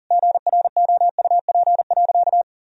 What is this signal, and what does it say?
Morse code, heard as a keyed tone, about 40 wpm, 700 Hz. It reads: GROUP.
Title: Sentences from Top 1000 Words - Review Part 1 - 40wpm